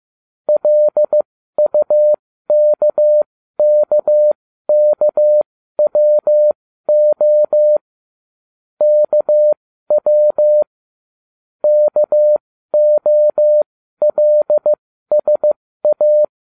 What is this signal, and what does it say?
Morse code, heard as a keyed tone, about 15 wpm, spaced as for 12 wpm, 600 Hz. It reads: LUKKKWO KW KOLSA